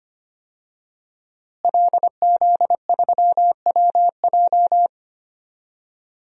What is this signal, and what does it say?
Morse code, heard as a keyed tone, about 25 wpm, 700 Hz. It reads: LZ3WJ
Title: DXCC-100 Call Signs - 5 Characters - Repeated 2x - 25wpm